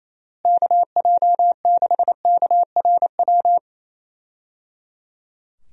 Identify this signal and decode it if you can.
Morse code, heard as a keyed tone, about 28 wpm, 700 Hz. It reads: KJ6KRW
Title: USA Call Signs - 6 Characters - Repeated 3x - 28wpm